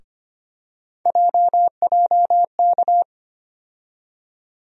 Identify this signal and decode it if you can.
Morse code, heard as a keyed tone, about 25 wpm, 700 Hz. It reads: JJK
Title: USA Call Sign Postfixes - 3 Letters 25wpm